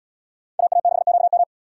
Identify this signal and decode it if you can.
Morse code, heard as a keyed tone, about 77 wpm, 700 Hz. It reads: AA6QM